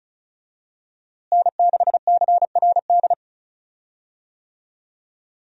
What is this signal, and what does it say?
Morse code, heard as a keyed tone, about 34 wpm, 700 Hz. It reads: N6CRD
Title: USA Call Signs - 5 Characters - Repeated 3x - 35wpm